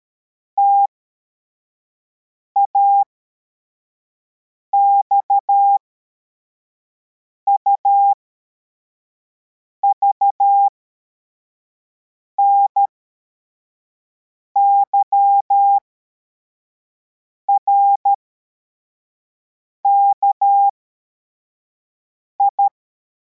Morse code, heard as a keyed tone, about 13 wpm, spaced as for 4 wpm, 800 Hz. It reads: TAXUV NYRKI
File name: formation / cw / Text52